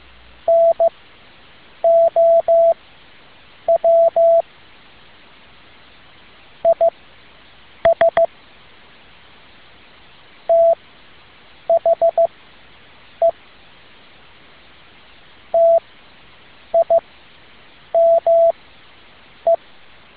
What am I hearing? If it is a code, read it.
Morse code, heard as a keyed tone, about 15 wpm, spaced as for 7 wpm, 670 Hz. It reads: NOW IS THE TIME
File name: tin-upgrade / QSO 2013 02 12